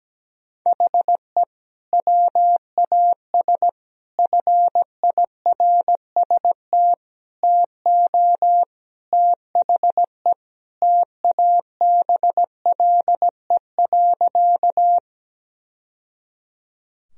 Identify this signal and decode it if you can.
Morse code, heard as a keyed tone, about 17 wpm, 700 Hz. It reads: HE WAS FIRST TO THE TABLE.
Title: Sentences from Top 1000 Words - Review Part 2 - 17wpm